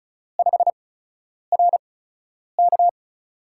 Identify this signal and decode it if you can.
Morse code, heard as a keyed tone, about 36 wpm, spaced as for 10 wpm, 700 Hz. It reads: 5RK
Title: USA Call Signs - 5 Characters - 8x Character Spacing - 35wpm